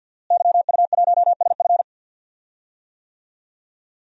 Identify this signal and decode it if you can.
Morse code, heard as a keyed tone, about 45 wpm, 700 Hz. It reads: YU1SF
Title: DXCC-100 Call Signs - 5 Characters - Repeated 2x - 50wpm